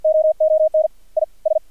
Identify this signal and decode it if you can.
Morse code, heard as a keyed tone, about 56 wpm, spaced as for 41 wpm, 630 Hz, with radio noise. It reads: OON IS
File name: archived / CRSnet